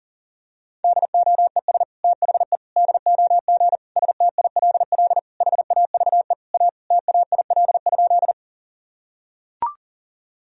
Morse code, heard as a keyed tone, about 40 wpm, 700 Hz. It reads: DOES THE DOG STILL HAVE A TAIL?
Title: Sentences from Top 1000 Words - Review Part 3 - 40wpm